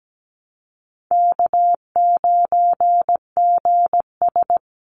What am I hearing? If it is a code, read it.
Morse code, heard as a keyed tone, about 17 wpm, 700 Hz. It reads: K9GS
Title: Call Signs - 17wpm